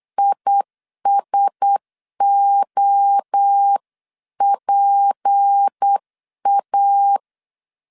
Morse code, heard as a keyed tone, about 8 wpm, 800 Hz. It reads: ISOPA
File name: CW / DET / 1 9 1500C S B 8wpm 5624